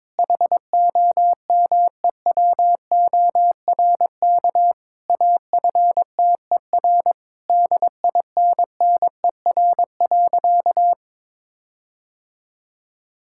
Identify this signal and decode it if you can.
Morse code, heard as a keyed tone, about 22 wpm, 700 Hz. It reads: HOMEWORK AFTER DINNER.
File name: Difficult Sentences with Hints - 22wpm